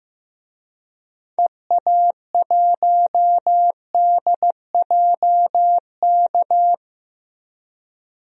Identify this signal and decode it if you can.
Morse code, heard as a keyed tone, about 15 wpm, 700 Hz. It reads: EA1DJK
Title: DXCC-100 Call Signs - 6 Characters - 15wpm